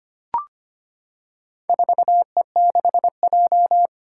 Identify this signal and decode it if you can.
Morse code, heard as a keyed tone, about 25 wpm, 700 Hz. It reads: 4E6J
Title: Four Challenging Letter-Number Combinations 25wpm